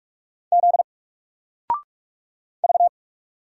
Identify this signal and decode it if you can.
Morse code, heard as a keyed tone, about 44 wpm, 700 Hz. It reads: Z V
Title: Single Letter-Number 45wpm